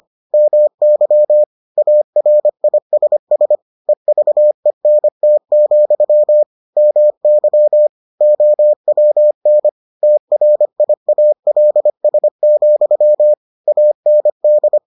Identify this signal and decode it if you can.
Morse code, heard as a keyed tone, about 25 wpm, 600 Hz. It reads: MY ARISS EVENT, MY OWN TRIALS, AND